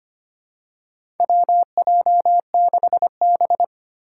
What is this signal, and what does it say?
Morse code, heard as a keyed tone, about 25 wpm, 700 Hz. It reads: WJ6B
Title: Four Challenging Letter-Number Combinations 25wpm